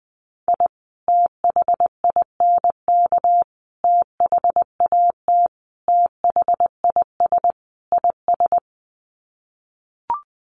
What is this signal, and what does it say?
Morse code, heard as a keyed tone, about 20 wpm, 700 Hz. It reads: I THINK THAT THIS IS
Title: Sets of 5 Words Encore2 20wpm